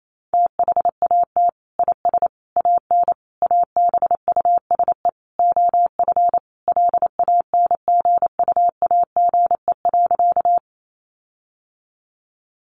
Morse code, heard as a keyed tone, about 28 wpm, 700 Hz. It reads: THAT IS AN ABUSE OF LANGUAGE.
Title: Sentences from Top 1300 Words 28wpm